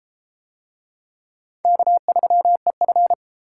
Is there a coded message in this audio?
Yes: Morse code from 1.6 to 3.1 s, heard as a keyed tone, about 28 wpm, 700 Hz. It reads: K3EF